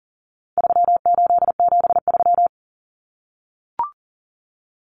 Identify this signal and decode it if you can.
Morse code, heard as a keyed tone, about 40 wpm, 700 Hz. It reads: 3873
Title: Numbers - 4-Digit 40wpm